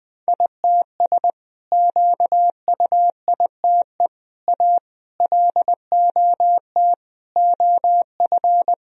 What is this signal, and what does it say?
Morse code, heard as a keyed tone, about 21 wpm, 700 Hz. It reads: ITS QUITE A LOT OF
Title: Sentences from Top 1000 Words - Review Part 3 - 20wpm